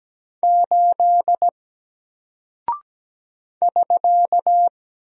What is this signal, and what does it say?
Morse code, heard as a keyed tone, about 17 wpm, 700 Hz. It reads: 8 <SK>
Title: Single Character (Letter:Number:Punctuation:Prosign) Alt 17wpm